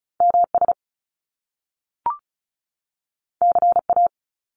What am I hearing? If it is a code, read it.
Morse code, heard as a keyed tone, about 36 wpm, 700 Hz. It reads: MS CA
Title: US States 35wpm